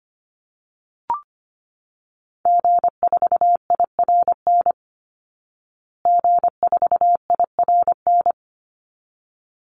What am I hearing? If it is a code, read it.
Morse code, heard as a keyed tone, about 25 wpm, 700 Hz. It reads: G4IRN G4IRN
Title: Call Signs Repeated 3x - 25wpm